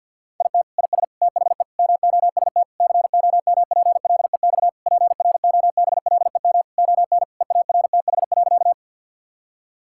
Morse code, heard as a keyed tone, about 50 wpm, 700 Hz. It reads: IT IS THE MOST COMPLEX PROBLEM ON EARTH.
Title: Sentences from Top 1000 Words - Review Part 6 - 50wpm